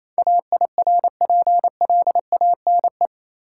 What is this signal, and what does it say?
Morse code, heard as a keyed tone, about 27 wpm, 700 Hz. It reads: AIRPLANE
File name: Easy Words - Speed-Racing with Hints - 22wpm